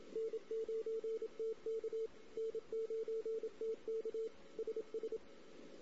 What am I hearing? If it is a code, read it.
Morse code, heard as a keyed tone, about 26 wpm, 440 Hz, with radio noise. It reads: N9TK N9TK SS
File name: CW / 2008-ss-cw-k5tr-op-n5rz -16:00